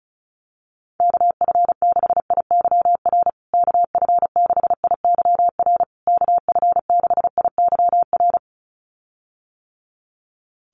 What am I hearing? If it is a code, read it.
Morse code, heard as a keyed tone, about 34 wpm, 700 Hz. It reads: KF6IYR KF6IYR KF6IYR
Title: USA Call Signs Repeated 3x 35wpm